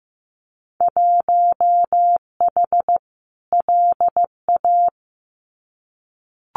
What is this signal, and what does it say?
Morse code, heard as a keyed tone, about 15 wpm, 700 Hz. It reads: 1H LA
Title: Winter Field Day Exchanges 15wpm